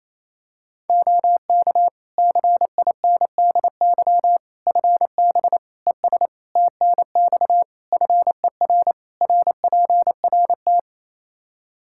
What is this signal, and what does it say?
Morse code, heard as a keyed tone, about 28 wpm, 700 Hz. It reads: OK CINDY FB ES TNX FER RPRT
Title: K1USN SST QSO Practice 28wpm